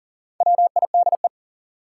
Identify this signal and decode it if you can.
Morse code, heard as a keyed tone, about 40 wpm, 700 Hz. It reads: WIDE
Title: Sentences from Top 900 Words - New Word Drill - 40wpm